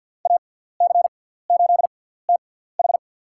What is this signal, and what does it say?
Morse code, heard as a keyed tone, about 52 wpm, spaced as for 17 wpm, 700 Hz. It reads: AC8TH
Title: USA Call Signs - 6x Character Spacing - 50wpm